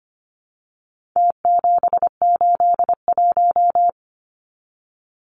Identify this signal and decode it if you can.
Morse code, heard as a keyed tone, about 25 wpm, 700 Hz. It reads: T781
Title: Numbers - Contest Style - 4-Digit 25wpm